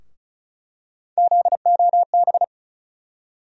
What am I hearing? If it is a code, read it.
Morse code, heard as a keyed tone, about 34 wpm, 700 Hz. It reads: ZOB
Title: USA Call Sign Postfixes - 3 Letters 35wpm